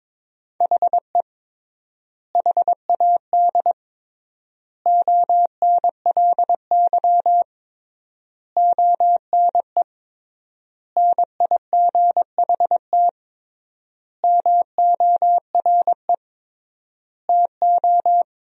Morse code, heard as a keyed tone, about 22 wpm, 700 Hz. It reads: HE HAD ONLY ONE NIGHT MORE TO
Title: 3x Word-Spacing - Sentences from Top 300 Words - 22wpm